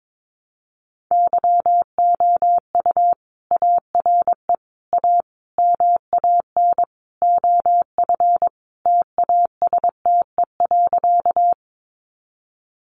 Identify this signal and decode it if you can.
Morse code, heard as a keyed tone, about 22 wpm, 700 Hz. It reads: YOU ARE A MAN OF TASTE.
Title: Sentences from Top 1100 Words 22wpm